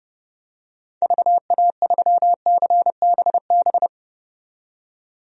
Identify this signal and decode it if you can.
Morse code, heard as a keyed tone, about 30 wpm, 700 Hz. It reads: VA3CBB
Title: DXCC-100 Call Signs - 6 Characters - 30wpm